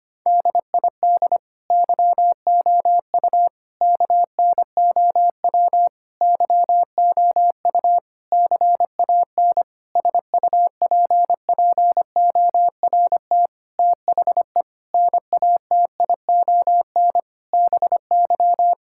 Morse code, heard as a keyed tone, about 25 wpm, 700 Hz. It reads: DID YOU KNOW YOU CAN SUPPORT THE NATION BY
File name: practice / 25wpm / Sentences with 9-Word Phrases - 25wpm